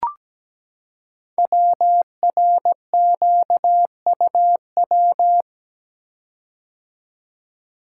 Morse code, heard as a keyed tone, about 17 wpm, 700 Hz. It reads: WRQUW
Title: Code Groups - 5 Letters - 17wpm